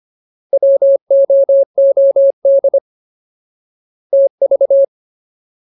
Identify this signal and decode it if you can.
Morse code, heard as a keyed tone, about 25 wpm, 550 Hz. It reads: WOOD TV
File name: Ultimate Word List / 25WPM Ultimate